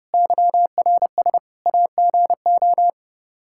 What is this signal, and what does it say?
Morse code, heard as a keyed tone, about 32 wpm, 700 Hz. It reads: YRS AGO
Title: Practice QSO - Exchange 3 - More Info - 30wpm